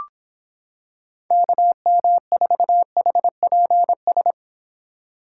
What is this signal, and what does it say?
Morse code, heard as a keyed tone, about 26 wpm, 700 Hz. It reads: KM4HPS